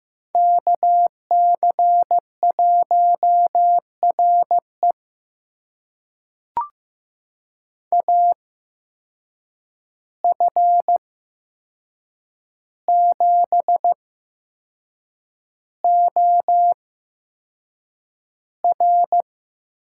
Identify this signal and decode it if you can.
Morse code, heard as a keyed tone, about 15 wpm, spaced as for 6 wpm, 700 Hz. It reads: KC1RE AF7OR
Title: USA Call Signs - 5 Characters - 8x Character Spacing - 15wpm